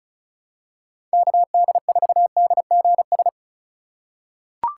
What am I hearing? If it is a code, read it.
Morse code, heard as a keyed tone, about 36 wpm, 700 Hz. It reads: KD4DGS
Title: USA Call Signs - 6 Characters - 2x Character Spacing - 35wpm